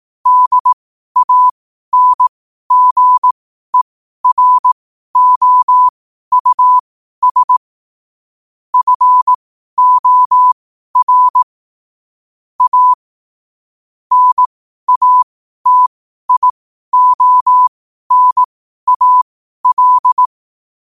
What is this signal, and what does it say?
Morse code, heard as a keyed tone, about 18 wpm, spaced as for 12 wpm, 1000 Hz. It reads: DANGEROUS FOR A NATIONAL
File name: qotdfiles / 12wpm